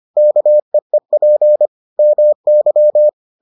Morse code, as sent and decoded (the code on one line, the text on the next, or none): -.- . .--. / -- -.--
KEEP MY